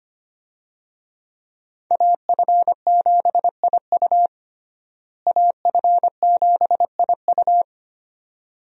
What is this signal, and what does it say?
Morse code, heard as a keyed tone, about 25 wpm, 700 Hz. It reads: AF7IU AF7IU
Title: USA Call Signs - 5 Characters - Repeated 2x - 25wpm